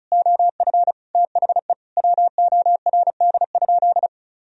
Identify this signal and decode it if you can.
Morse code, heard as a keyed tone, about 36 wpm, 700 Hz. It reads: OF THE WORD?
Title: Sentences from Top 1000 Words - Review Part 4 - 35wpm